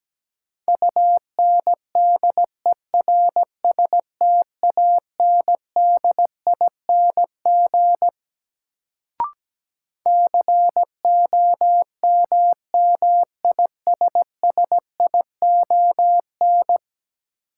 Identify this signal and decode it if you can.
Morse code, heard as a keyed tone, about 17 wpm, 700 Hz. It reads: UNDERSTANDING COMMISSION
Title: Sentences from Top 1400 Words - New Word Drill Alt - 17wpm